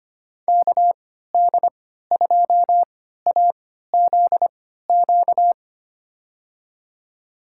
Morse code, heard as a keyed tone, about 25 wpm, spaced as for 14 wpm, 700 Hz. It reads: KD2AZQ
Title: USA Call Signs - 6 Characters - 3x Character Spacing - 25wpm